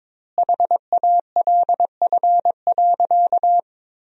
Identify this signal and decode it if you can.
Morse code, heard as a keyed tone, about 22 wpm, 700 Hz. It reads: HALF.
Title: 2x Word-Spacing - Sentences from Top 400 Words - 22wpm